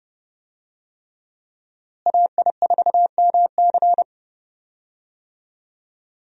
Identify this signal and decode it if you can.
Morse code, heard as a keyed tone, about 31 wpm, 700 Hz. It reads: AI4MC
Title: USA Call Signs - 5 Characters - Repeated 3x - 30wpm